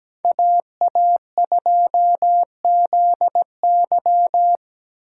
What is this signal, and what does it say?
Morse code, heard as a keyed tone, about 17 wpm, 700 Hz. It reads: AA2ZY
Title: USA Call Signs - 5 Characters - 8x Character Spacing - 17wpm